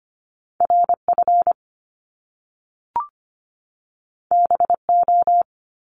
Morse code, heard as a keyed tone, about 25 wpm, 700 Hz. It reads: RF BO
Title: Top Two Letter Combinations in Words 25wpm